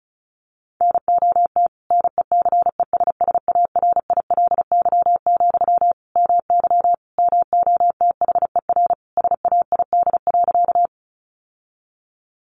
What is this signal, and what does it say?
Morse code, heard as a keyed tone, about 34 wpm, 700 Hz. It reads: NOT NECESSARILY, MY MOTHER SAID.
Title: Sentences from Top 1400 Words 35wpm